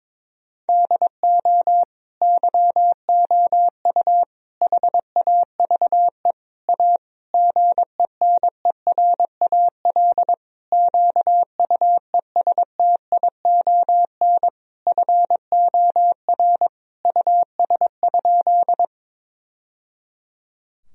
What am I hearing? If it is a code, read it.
Morse code, heard as a keyed tone, about 22 wpm, 700 Hz. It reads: DO YOU HAVE A GENERAL QUESTION FOR US?
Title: Sentences from Top 1000 Words - Review Part 3 - 22wpm